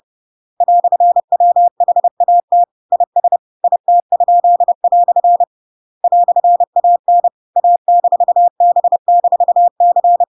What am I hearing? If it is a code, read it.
Morse code, heard as a keyed tone, about 30 wpm, 700 Hz. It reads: "WHAT IS IT?" "AN A-B-C
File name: Pinocchio / Pinocchio ch8 30 30 ik1hge